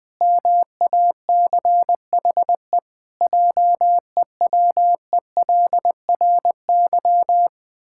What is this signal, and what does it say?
Morse code, heard as a keyed tone, about 21 wpm, 700 Hz. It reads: MACHE JEWELRY
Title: Pangram Challenge v2 - 20wpm